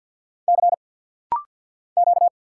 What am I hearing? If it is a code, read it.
Morse code, heard as a keyed tone, about 50 wpm, 690 Hz. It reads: C Q
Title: Single Letter-Number Alt - Rapid-Fire 50wpm